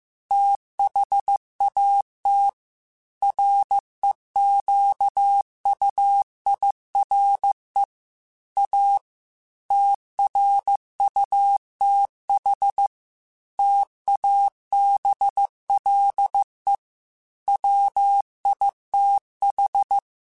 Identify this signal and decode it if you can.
Morse code, heard as a keyed tone, about 15 wpm, 800 Hz. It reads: THAT REQUIRE A TRUTH TABLE WITH
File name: Technical Subjects / Technical Subjects-mp3s / AND Gate @13